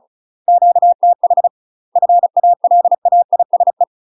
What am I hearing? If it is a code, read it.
Morse code, heard as a keyed tone, about 36 wpm, 700 Hz. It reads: QTH FALAISE
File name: UK MASTER QSOs TRAINING / 35WPM / UK MASTER QSOs 0018